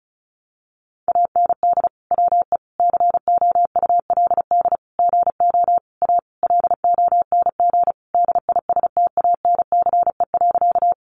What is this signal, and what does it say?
Morse code, heard as a keyed tone, about 34 wpm, 700 Hz. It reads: AND WE COULD GO A LONG DISTANCE.
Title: Sentences from Top 1200 Words 35wpm